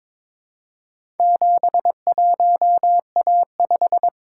Morse code, heard as a keyed tone, about 21 wpm, 700 Hz. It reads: 71A5
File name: practice / 22wpm / Four Letter-Number Combinations - 22wpm